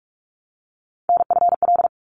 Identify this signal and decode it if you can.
Morse code, heard as a keyed tone, about 46 wpm, 700 Hz. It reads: NFL
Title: ARRL Sections 45wpm